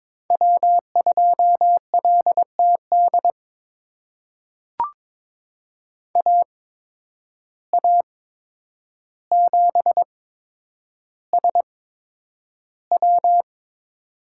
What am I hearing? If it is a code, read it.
Morse code, heard as a keyed tone, about 22 wpm, spaced as for 9 wpm, 700 Hz. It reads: W2LTD AA7SW
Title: USA Call Signs - 5 Characters - 8x Character Spacing - 22wpm